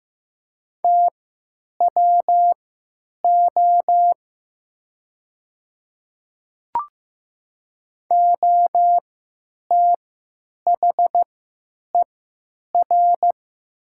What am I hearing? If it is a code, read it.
Morse code, heard as a keyed tone, about 15 wpm, spaced as for 9 wpm, 700 Hz. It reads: TWO OTHER
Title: 3x Character Spacing - Top 100 Words - 15wpm